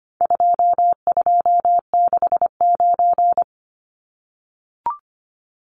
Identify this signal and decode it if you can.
Morse code, heard as a keyed tone, about 25 wpm, 700 Hz. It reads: 2269
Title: Numbers - 4-Digit 25wpm